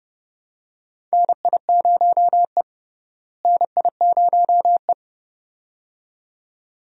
Morse code, heard as a keyed tone, about 32 wpm, 700 Hz. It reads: NI0E NI0E
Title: USA Call Signs - 4 Characters - Repeated 2x - 30wpm